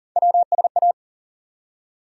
Morse code, heard as a keyed tone, about 40 wpm, 700 Hz. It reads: WSA
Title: USA Call Sign Postfixes - 3 Letters 40wpm